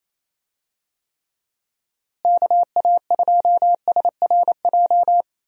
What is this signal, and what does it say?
Morse code, heard as a keyed tone, about 28 wpm, 700 Hz. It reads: KA2SRJ
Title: USA Call Signs - 6 Characters - Repeated 3x - 28wpm